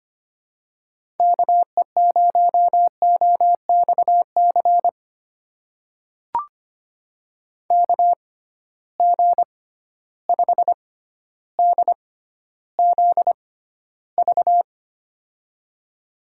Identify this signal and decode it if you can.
Morse code, heard as a keyed tone, about 25 wpm, spaced as for 13 wpm, 700 Hz. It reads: KE0OXC KG5DZV
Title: USA Call Signs - 6 Characters - 6x Character Spacing - 25wpm